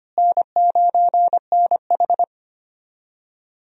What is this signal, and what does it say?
Morse code, heard as a keyed tone, about 25 wpm, 700 Hz. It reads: N9NH
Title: USA Call Signs - 4 Characters - Repeated 3x - 25wpm